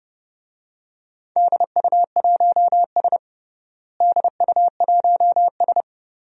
Morse code, heard as a keyed tone, about 30 wpm, 700 Hz. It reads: DU1S DU1S
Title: DXCC-100 Call Signs - 4 Characters - Repeated 2x - 30wpm